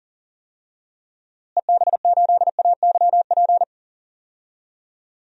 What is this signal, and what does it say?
Morse code, heard as a keyed tone, about 40 wpm, 700 Hz. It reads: EB8AYP
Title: DXCC-100 Call Signs - 6 Characters - Repeated 3x - 40wpm